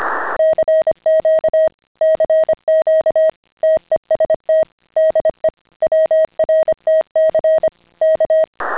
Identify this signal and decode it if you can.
Morse code, heard as a keyed tone, about 25 wpm, 620 Hz. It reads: CQ CQ TEST DE WRTC K
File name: CW-pile up